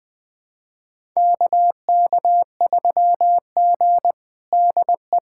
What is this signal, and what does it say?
Morse code, heard as a keyed tone, about 20 wpm, 700 Hz. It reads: KK3G DE
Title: USA Callsign DE Callsign - 4 Characters - 20wpm